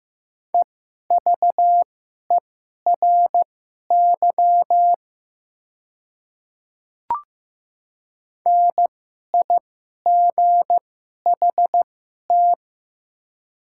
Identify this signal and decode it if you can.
Morse code, heard as a keyed tone, about 15 wpm, spaced as for 11 wpm, 700 Hz. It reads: EVERY NIGHT
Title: 2x Character Spacing - Top 300 Words - 15wpm